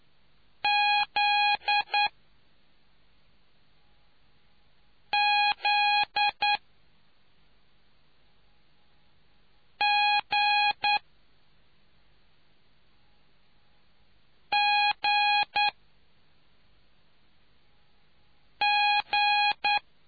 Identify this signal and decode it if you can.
Morse code, heard as a keyed tone, about 9 wpm, 830 Hz. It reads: Z Z G G G